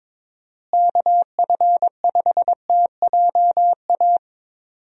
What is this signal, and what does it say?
Morse code, heard as a keyed tone, about 22 wpm, 700 Hz. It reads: KF5TJA
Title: USA Call Signs - 6 Characters - Repeated 2x - 22wpm